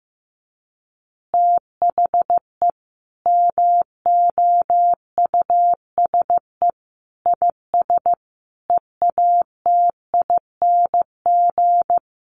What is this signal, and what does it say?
Morse code, heard as a keyed tone, about 15 wpm, 700 Hz. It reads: THE MOUSE IS EATING
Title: Easy Phrases 15wpm